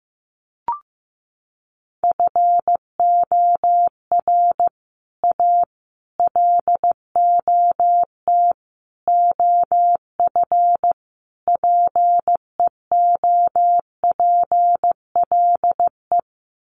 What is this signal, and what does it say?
Morse code, heard as a keyed tone, about 15 wpm, 700 Hz. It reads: FOR A LOT OF PEOPLE
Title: Sets of 5 Words Encore 15wpm